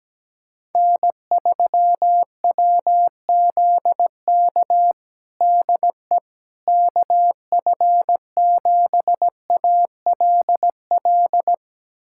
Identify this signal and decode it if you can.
Morse code, heard as a keyed tone, about 17 wpm, 700 Hz. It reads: N3WZK DE KF7ALL
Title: USA Callsign DE Callsign 17wpm